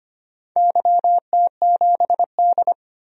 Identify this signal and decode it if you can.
Morse code, heard as a keyed tone, about 25 wpm, 700 Hz. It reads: YT7D
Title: DXCC-100 Call Signs - 4 Characters - 25wpm